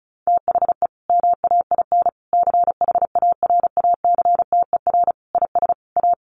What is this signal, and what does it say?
Morse code, heard as a keyed tone, about 34 wpm, 700 Hz. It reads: THE MAIN CHARACTER IS A
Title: Additional Easy Phrases 35wpm